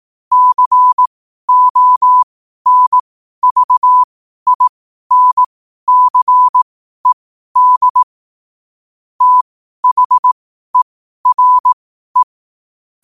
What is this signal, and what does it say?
Morse code, heard as a keyed tone, about 18 wpm, spaced as for 12 wpm, 1000 Hz. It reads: CONVINCED THERE
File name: 12wpm